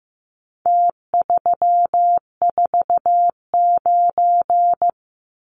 Morse code, heard as a keyed tone, about 15 wpm, 700 Hz. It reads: T349
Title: Numbers - Contest Style - 4-Digit 15wpm